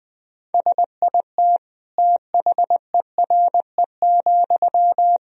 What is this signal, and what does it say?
Morse code, heard as a keyed tone, about 20 wpm, 700 Hz. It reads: SIT THERE,
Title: Sentences from Top 1000 Words - Review Part 1 - 20wpm